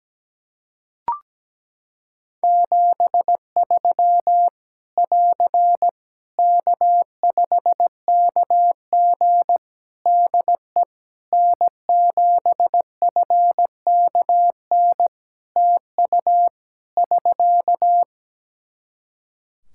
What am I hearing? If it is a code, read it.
Morse code, heard as a keyed tone, about 17 wpm, 700 Hz. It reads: 73 <AR> K5KG DE N7FKN TU <SK>
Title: K1USN SST QSO Practice 17wpm